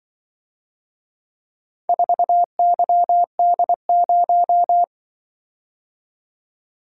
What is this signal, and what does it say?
Morse code, heard as a keyed tone, about 24 wpm, 700 Hz. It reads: 4YD0
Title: Speed-Racing - Four Letter-Number Combinations - 20wpm